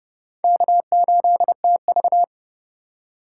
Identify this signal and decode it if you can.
Morse code, heard as a keyed tone, about 29 wpm, 700 Hz. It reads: K8TV